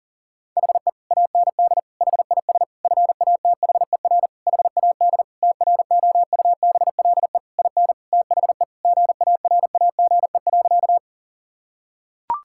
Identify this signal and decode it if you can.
Morse code, heard as a keyed tone, about 40 wpm, 700 Hz. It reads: HE AND HIS FATHER HAD TROUBLE IN THE GARAGE.
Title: Sentences from Top 1600 Words 40wpm